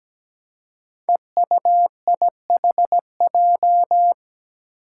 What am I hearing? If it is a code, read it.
Morse code, heard as a keyed tone, about 17 wpm, 700 Hz. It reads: EUIHJ
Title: Five Challenging Letter-Number Combinations 17wpm